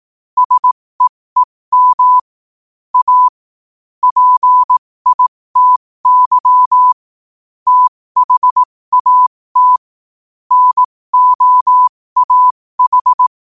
Morse code, heard as a keyed tone, about 18 wpm, spaced as for 15 wpm, 1000 Hz. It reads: SEEM A PITY THAT NOAH
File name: qotdfiles / 15wpm